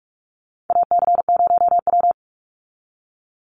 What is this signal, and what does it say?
Morse code, heard as a keyed tone, about 44 wpm, 700 Hz. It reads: AC0W
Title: Call Signs Repeated 2x - 45wpm